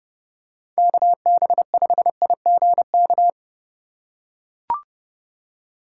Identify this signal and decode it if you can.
Morse code, heard as a keyed tone, about 31 wpm, 700 Hz. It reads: KB5IGK